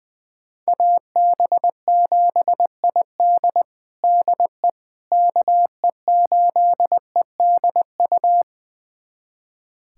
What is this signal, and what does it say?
Morse code, heard as a keyed tone, about 20 wpm, 700 Hz. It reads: AB7ID DE KE8EDU
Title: USA Callsign DE Callsign 20wpm